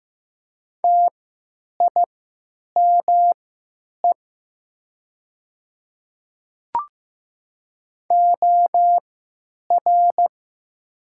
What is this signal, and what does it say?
Morse code, heard as a keyed tone, about 15 wpm, spaced as for 9 wpm, 700 Hz. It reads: TIME OR
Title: 3x Character Spacing - Top 100 Words - 15wpm